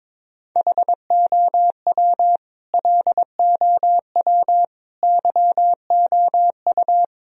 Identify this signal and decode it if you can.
Morse code, heard as a keyed tone, about 22 wpm, 700 Hz. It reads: HOW LOW YOU
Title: Sentences from Top 1000 Words - Review Part 1 - 22wpm